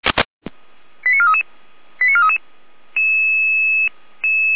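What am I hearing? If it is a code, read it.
Five-tone selective call (ZVEI), heard as quick runs of five tones, tones of about 70 ms. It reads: 89433 89433
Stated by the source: ZVEI five-tone selective call from a fire brigade, received on SDR.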